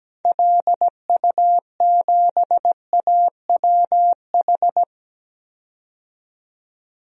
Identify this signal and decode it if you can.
Morse code, heard as a keyed tone, about 17 wpm, 700 Hz. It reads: LU7AWH
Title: DXCC-100 Call Signs - 6 Characters - 17wpm